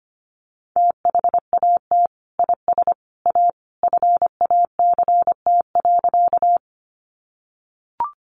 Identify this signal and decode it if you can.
Morse code, heard as a keyed tone, about 25 wpm, 700 Hz. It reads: THAT IS A FACT.
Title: Sentences from Top 500 Words 25wpm